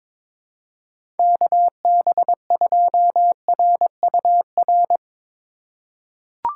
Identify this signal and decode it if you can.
Morse code, heard as a keyed tone, about 22 wpm, 700 Hz. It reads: KB2RUR
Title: USA Call Signs - 3x Character Spacing - 22wpm